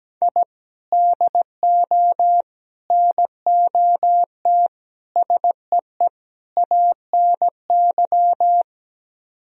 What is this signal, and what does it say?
Morse code, heard as a keyed tone, about 17 wpm, 700 Hz. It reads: I DO NOT SEE ANY
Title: Speed-Racing - Sets of 5 Words 17wpm